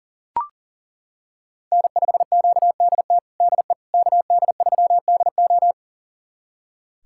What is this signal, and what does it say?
Morse code, heard as a keyed tone, about 40 wpm, 700 Hz. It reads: N5QDT DE KD3DO
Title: USA Callsign DE Callsign - 5 Characters - 40wpm